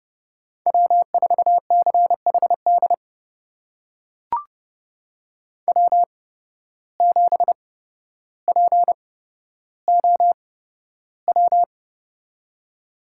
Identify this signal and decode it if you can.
Morse code, heard as a keyed tone, about 30 wpm, spaced as for 13 wpm, 700 Hz. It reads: W4CHD W7POW
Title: USA Call Signs - 5 Characters - 8x Character Spacing - 30wpm